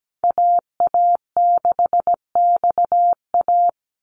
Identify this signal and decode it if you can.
Morse code, heard as a keyed tone, about 17 wpm, 700 Hz. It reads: AA6XA
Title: NA SOTA Activator Callsigns 17wpm